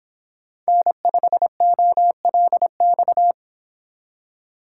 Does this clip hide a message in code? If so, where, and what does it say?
Yes: Morse code from 0.7 to 3.3 s, heard as a keyed tone, about 26 wpm, 700 Hz. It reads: N5OLX